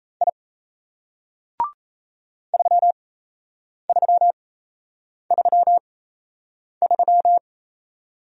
Morse code, heard as a keyed tone, about 36 wpm, spaced as for 8 wpm, 700 Hz. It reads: I 3333